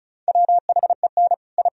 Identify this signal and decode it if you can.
Morse code, heard as a keyed tone, about 46 wpm, spaced as for 41 wpm, 700 Hz. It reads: WHEN I
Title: Speed-Racing - Sets of 5 Words 35wpm